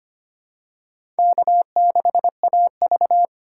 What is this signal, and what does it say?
Morse code, heard as a keyed tone, about 25 wpm, 700 Hz. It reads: K6AV
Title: USA Call Signs - 2x Character Spacing - 25wpm